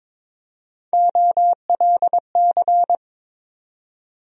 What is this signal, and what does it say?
Morse code, heard as a keyed tone, about 21 wpm, 700 Hz. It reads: OLC
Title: USA Call Sign Postfixes - 3 Letters 22wpm